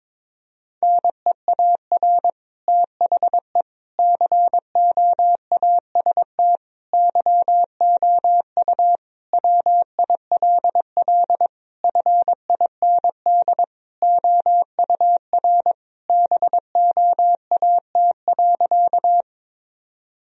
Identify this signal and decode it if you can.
Morse code, heard as a keyed tone, about 22 wpm, 700 Hz. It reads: NEAR THE COAST YOU WILL FIND OUR BOAT.
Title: Sentences from Top 1000 Words - Review Part 4 - 22wpm